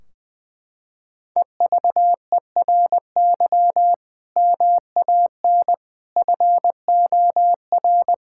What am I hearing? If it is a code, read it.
Morse code, heard as a keyed tone, about 20 wpm, 700 Hz. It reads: EVERY MAN FOR
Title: Sentences from Top 1600 Words 20wpm